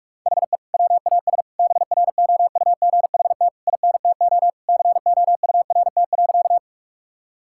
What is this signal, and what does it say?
Morse code, heard as a keyed tone, about 46 wpm, 700 Hz. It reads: HE WAS BROUGHT INTO COURT.
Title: Sentences from Top 1600 Words 45wpm